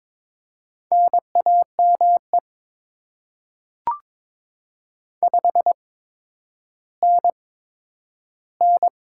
Morse code, heard as a keyed tone, about 22 wpm, spaced as for 11 wpm, 700 Hz. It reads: NAME 5NN
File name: Top 20 QSO Elements - 8x Character Spacing - 22wpm